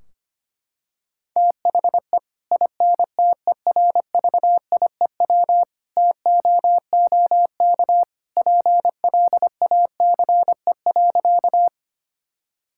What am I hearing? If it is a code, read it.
Morse code, heard as a keyed tone, about 25 wpm, 700 Hz. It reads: THE INTERVIEW TOOK PLACE.
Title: Sentences from Top 1600 Words 25wpm